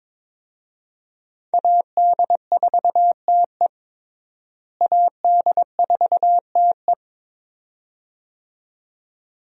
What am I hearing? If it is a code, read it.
Morse code, heard as a keyed tone, about 22 wpm, 700 Hz. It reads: AD4TE AD4TE
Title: USA Call Signs - 5 Characters - Repeated 2x - 22wpm